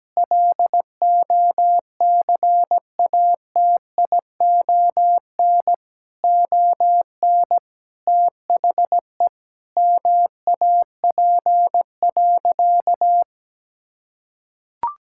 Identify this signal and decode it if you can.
Morse code, heard as a keyed tone, about 17 wpm, 700 Hz. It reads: LOCATION ON THE MAP.
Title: Sentences from Top 1000 Words - Review Part 6 - 17wpm